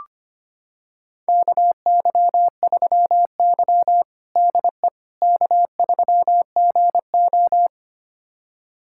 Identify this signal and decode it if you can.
Morse code, heard as a keyed tone, about 25 wpm, 700 Hz. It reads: KY3Y DE K3GO
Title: USA Callsign DE Callsign - 4 Characters - 25wpm